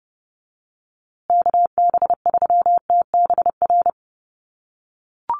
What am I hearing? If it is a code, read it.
Morse code, heard as a keyed tone, about 30 wpm, 700 Hz. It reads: KB3TBR
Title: USA Call Signs Repeated 3x 30wpm